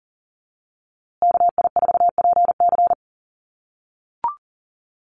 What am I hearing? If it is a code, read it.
Morse code, heard as a keyed tone, about 40 wpm, 700 Hz. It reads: KI4PC
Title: USA Call Signs Repeated 3x 40wpm